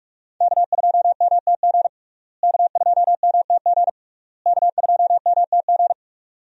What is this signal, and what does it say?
Morse code, heard as a keyed tone, about 46 wpm, 700 Hz. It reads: K2MTG K2MTG K2MTG
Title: USA Call Signs - 5 Characters - Repeated 3x - 45wpm